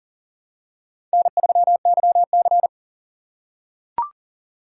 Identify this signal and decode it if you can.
Morse code, heard as a keyed tone, about 40 wpm, 700 Hz. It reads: N3YC
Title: USA Call Signs - 4 Characters - 6x Character Spacing - 40wpm